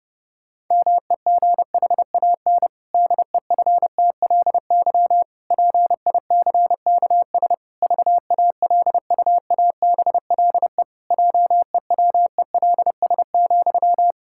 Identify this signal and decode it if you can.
Morse code, heard as a keyed tone, about 32 wpm, 700 Hz. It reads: MEGHAN DEFTLY PICKS VALUABLE JEWELS,
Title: Pangram Challenge v2 - 30wpm